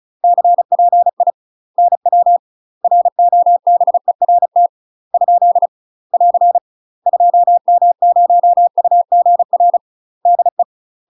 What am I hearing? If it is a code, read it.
Morse code, heard as a keyed tone, about 36 wpm, 700 Hz. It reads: CPI NW ROBERT ? <AR> 2M0UGR DE
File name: UK MASTER QSOs TRAINING/35WPM